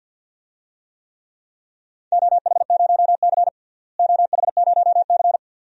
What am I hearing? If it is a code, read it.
Morse code, heard as a keyed tone, about 48 wpm, 700 Hz. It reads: OH0C OH0C
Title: DXCC-100 Call Signs - 4 Characters - Repeated 2x - 50wpm